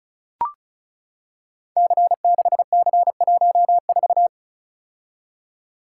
Morse code, heard as a keyed tone, about 36 wpm, 700 Hz. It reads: C6C14
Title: Code Groups - 5 Letter:Numbers - 35wpm